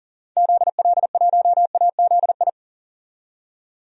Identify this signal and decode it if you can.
Morse code, heard as a keyed tone, about 40 wpm, 700 Hz. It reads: ZL1AZI
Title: DXCC-100 Call Signs - 6 Characters - 40wpm